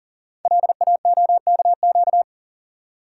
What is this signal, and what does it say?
Morse code, heard as a keyed tone, about 37 wpm, 700 Hz. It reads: LAOKQ
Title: Code Groups - 5 Letters - 40wpm